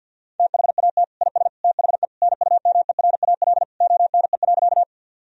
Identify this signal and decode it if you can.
Morse code, heard as a keyed tone, about 49 wpm, 700 Hz. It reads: THAT IS THE NUMERAL ONE.
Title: Sentences from Top 1000 Words - Review Part 2 - 50wpm